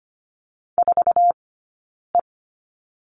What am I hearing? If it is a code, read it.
Morse code, heard as a keyed tone, about 25 wpm, 700 Hz. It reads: 4 E
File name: course / 25:10wpm / 092 Lesson 16d - Letter c - Two Characters 25wpm